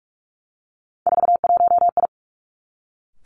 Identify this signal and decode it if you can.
Morse code, heard as a keyed tone, about 46 wpm, 700 Hz. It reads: 41I
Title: Three Challenging Letter-Number Combinations 45wpm